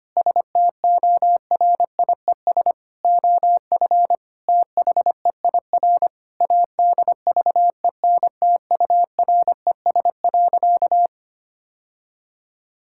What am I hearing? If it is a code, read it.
Morse code, heard as a keyed tone, about 25 wpm, 700 Hz. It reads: STORIES OF THEIR ADVENTURES.
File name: practice / 25wpm / Sentences with 2-Word Phrases - 25wpm